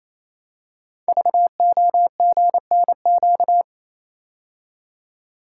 Morse code, heard as a keyed tone, about 28 wpm, 700 Hz. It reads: VOGNQ
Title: Code Groups - 5 Letters - 28wpm